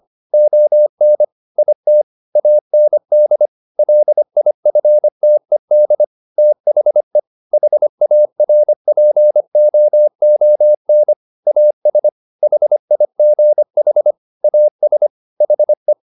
Morse code, heard as a keyed tone, about 25 wpm, 600 Hz. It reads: ON IT AND LIFTED THE HARPOON AS HIGH AS HE